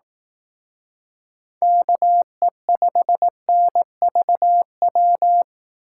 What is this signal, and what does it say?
Morse code, heard as a keyed tone, about 18 wpm, 700 Hz. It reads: KE5NVW
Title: Speed Racing - USA 2x3 Call Signs 15wpm